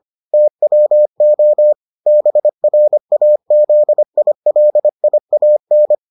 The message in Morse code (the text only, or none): TWO BRAZILIAN